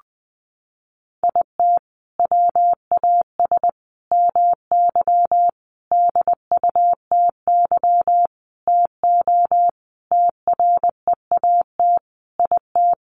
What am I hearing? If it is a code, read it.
Morse code, heard as a keyed tone, about 20 wpm, 700 Hz. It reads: IT WAS MY DUTY TO TREAT IT
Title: Sentences from Top 1500 Words 20wpm